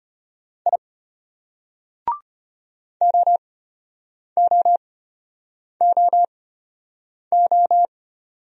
Morse code, heard as a keyed tone, about 33 wpm, spaced as for 8 wpm, 700 Hz. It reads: I OOOO